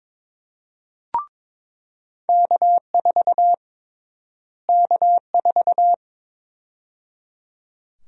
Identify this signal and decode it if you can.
Morse code, heard as a keyed tone, about 23 wpm, 700 Hz. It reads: K4 K4
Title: USA Call Sign Prefixes - Repeated 2x - 22wpm